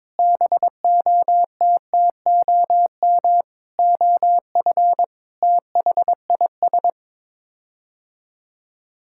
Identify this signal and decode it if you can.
Morse code, heard as a keyed tone, about 22 wpm, 700 Hz. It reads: BOTTOM OF THIS
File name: practice / 22wpm / Sets of 7 Words - 22wpm